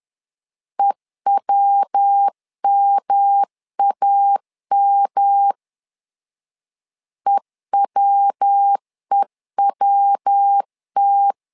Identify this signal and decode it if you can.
Morse code, heard as a keyed tone, about 10 wpm, 800 Hz. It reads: EWMAM EWEWT
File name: LEZIONI / 3 MWRO / 3 MW 10wpm 1959